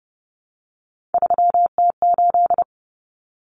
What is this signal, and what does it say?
Morse code, heard as a keyed tone, about 30 wpm, 700 Hz. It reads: 3T8
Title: Common Cut Numbers - 3-Digit 30wpm